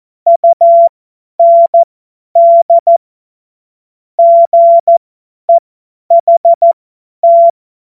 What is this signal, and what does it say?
Morse code, heard as a keyed tone, about 14 wpm, spaced as for 10 wpm, 680 Hz. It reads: UND GEHT
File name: Morsekurs / Witze / Witz 17.Oktober 2023 50 F70